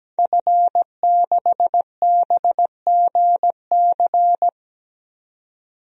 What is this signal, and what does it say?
Morse code, heard as a keyed tone, about 17 wpm, 700 Hz. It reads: F6BGC
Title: DXCC-100 Call Signs - 5 Characters - 17wpm